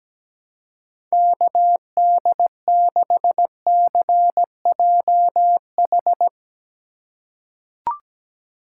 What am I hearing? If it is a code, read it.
Morse code, heard as a keyed tone, about 17 wpm, 700 Hz. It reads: KD6CJH
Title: USA Call Signs - 2x Character Spacing - 17wpm